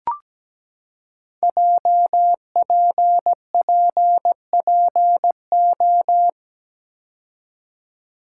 Morse code, heard as a keyed tone, about 17 wpm, 700 Hz. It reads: JPPPO
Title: Code Groups - 5 Letters - 17wpm